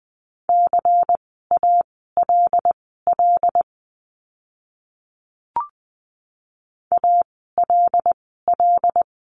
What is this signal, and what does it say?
Morse code, heard as a keyed tone, about 20 wpm, spaced as for 15 wpm, 700 Hz. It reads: CALL ALL
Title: 146 Lesson 24f - Letter y - Words 20wpm